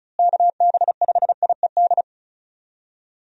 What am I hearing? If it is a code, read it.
Morse code, heard as a keyed tone, about 36 wpm, 700 Hz. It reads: KB5IED
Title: USA Call Signs - 6 Characters - Repeated 2x - 35wpm